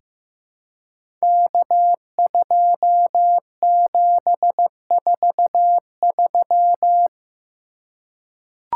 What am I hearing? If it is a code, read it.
Morse code, heard as a keyed tone, about 15 wpm, 700 Hz. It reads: K2743
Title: Code Groups - 5 Letter:Numbers - 15wpm